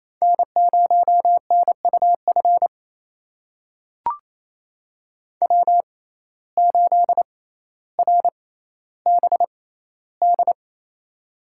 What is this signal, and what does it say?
Morse code, heard as a keyed tone, about 28 wpm, spaced as for 14 wpm, 700 Hz. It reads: N0NUF W8RBD
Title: USA Call Signs - 5 Characters - 6x Character Spacing - 28wpm